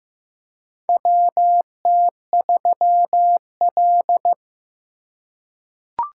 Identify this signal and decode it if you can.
Morse code, heard as a keyed tone, about 15 wpm, 700 Hz. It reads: WT3L
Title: USA Call Signs - 4 Characters - 2x Character Spacing - 15wpm